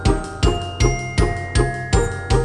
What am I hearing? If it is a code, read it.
There is no code.